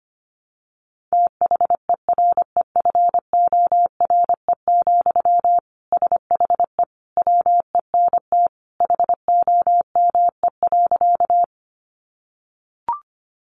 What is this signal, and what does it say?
Morse code, heard as a keyed tone, about 25 wpm, 700 Hz. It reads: THEREFORE, SHE WENT HOME.
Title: Sentences from Top 800 Words 25wpm